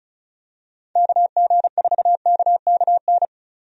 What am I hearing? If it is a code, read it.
Morse code, heard as a keyed tone, about 36 wpm, 700 Hz. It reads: KG4KKN
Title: USA Call Signs - 6 Characters - 35wpm